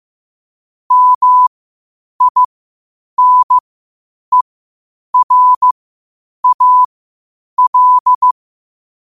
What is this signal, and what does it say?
Morse code, heard as a keyed tone, about 15 wpm, spaced as for 8 wpm, 1000 Hz. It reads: MINERAL
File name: qotdfiles / 8wpm